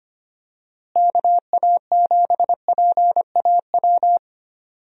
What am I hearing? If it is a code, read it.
Morse code, heard as a keyed tone, about 25 wpm, 700 Hz. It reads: KA7PAW
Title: USA Call Signs - 6 Characters - Repeated 3x - 25wpm